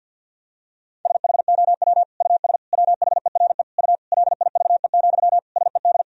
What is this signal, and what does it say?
Morse code, heard as a keyed tone, about 51 wpm, 700 Hz. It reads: SHOW US WHERE U LIVE, SED
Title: Sentences from Top 200 Words and Abbreviations 50wpm